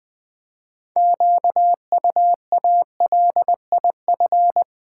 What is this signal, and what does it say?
Morse code, heard as a keyed tone, about 20 wpm, 700 Hz. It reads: QUALIF
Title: Build A Word - 3 Letters At A Time - 9-Letter Words - 20wpm